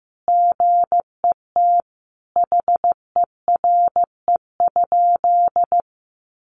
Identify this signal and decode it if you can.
Morse code, heard as a keyed tone, about 15 wpm, 700 Hz. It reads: GET HERE?
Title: Sentences from Top 200 Words 15wpm